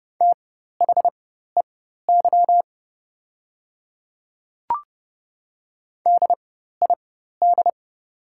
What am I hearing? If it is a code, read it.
Morse code, heard as a keyed tone, about 30 wpm, spaced as for 14 wpm, 690 Hz. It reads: THEY DID
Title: 4x Character Spacing - Top 100 Words - 30wpm